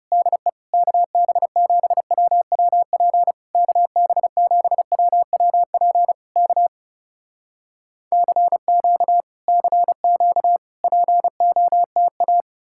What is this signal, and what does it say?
Morse code, heard as a keyed tone, about 36 wpm, 700 Hz. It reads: DE KB7WWP KB7WWP K CQ CQ POTA
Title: Speed-Racing Common QSO Phrases - 30wpm